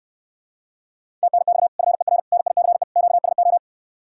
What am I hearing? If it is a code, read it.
Morse code, heard as a keyed tone, about 69 wpm, 700 Hz. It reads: TNX FER NICE QSO